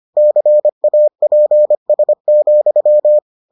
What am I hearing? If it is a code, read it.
Morse code, heard as a keyed tone, about 25 wpm, 600 Hz. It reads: CAPS,